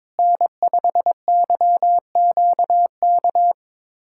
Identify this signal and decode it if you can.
Morse code, heard as a keyed tone, about 21 wpm, 700 Hz. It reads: N5YQK
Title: USA Callsign DE Callsign - 5 Characters - 22wpm